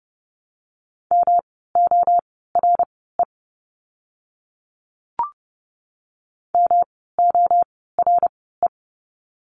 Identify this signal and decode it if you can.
Morse code, heard as a keyed tone, about 30 wpm, spaced as for 17 wpm, 700 Hz. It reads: MORE MORE
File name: course / 30:10wpm / 115 Lesson 19f - Letter w - Words 30wpm